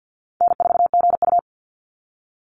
Morse code, heard as a keyed tone, about 50 wpm, 700 Hz. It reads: N4GU
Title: Call Signs - 50wpm